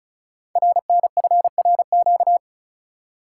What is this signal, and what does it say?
Morse code, heard as a keyed tone, about 36 wpm, 700 Hz. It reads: RNFRQ
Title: Code Groups - 5 Letters - 35wpm